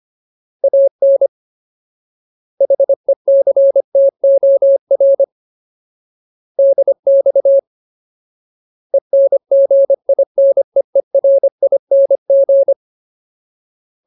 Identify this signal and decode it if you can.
Morse code, heard as a keyed tone, about 25 wpm, 550 Hz. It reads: AN HECTOR DX ENGINEERING
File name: Ultimate Word List / 25WPM Ultimate